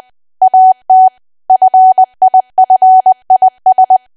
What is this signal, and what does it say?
Morse code, heard as a keyed tone, about 20 wpm, 750 Hz. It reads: AT FIFIS
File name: cwa / W1AW-Level3 / 20WPM